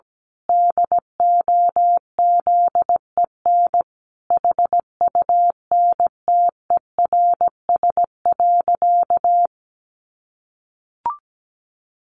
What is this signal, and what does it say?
Morse code, heard as a keyed tone, about 17 wpm, 700 Hz. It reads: DOZEN HUNTERS.
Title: A Sound Of Thunder 17wpm